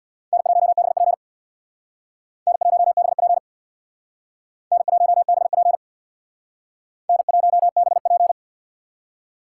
Morse code, heard as a keyed tone, about 60 wpm, 700 Hz. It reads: N1BP N1BP N1BP N1BP
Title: Speed Racing - USA 1x2 Call Signs 50wpm